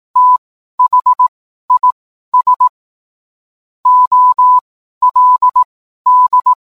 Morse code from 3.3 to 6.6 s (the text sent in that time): OLD